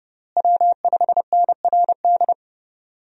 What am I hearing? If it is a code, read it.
Morse code, heard as a keyed tone, about 30 wpm, 700 Hz. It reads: W5NRD
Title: USA Call Signs - 5 Characters - Repeated 2x - 30wpm